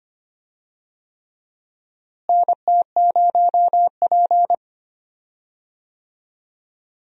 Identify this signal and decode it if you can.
Morse code, heard as a keyed tone, about 25 wpm, 700 Hz. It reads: NT0P
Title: USA Call Signs - 4 Characters - Repeated 3x - 25wpm